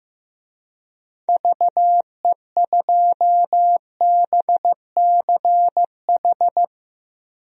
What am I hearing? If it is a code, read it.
Morse code, heard as a keyed tone, about 15 wpm, 700 Hz. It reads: VE2BCH
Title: DXCC-100 Call Signs - 6 Characters - Repeated 2x - 15wpm